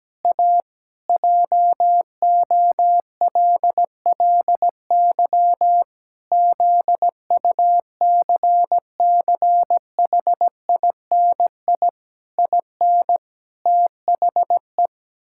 Morse code, heard as a keyed tone, about 17 wpm, 700 Hz. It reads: A JOLLY ZUCCHINI IN THE
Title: Pangram Challenge v2 - 17wpm